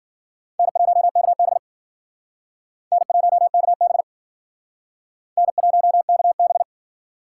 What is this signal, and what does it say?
Morse code, heard as a keyed tone, about 50 wpm, 700 Hz. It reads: N1KB N1KB N1KB
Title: Speed Racing - USA 1x2 Call Signs 40wpm